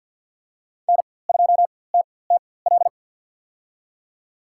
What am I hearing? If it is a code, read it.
Morse code, heard as a keyed tone, about 52 wpm, spaced as for 23 wpm, 700 Hz. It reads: N2TTL
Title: USA Call Signs - 5 Characters - 4x Character Spacing - 50wpm